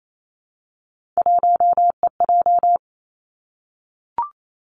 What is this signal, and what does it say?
Morse code, heard as a keyed tone, about 28 wpm, 700 Hz. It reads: 1EJ
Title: Three Challenging Letter-Number Combinations 28wpm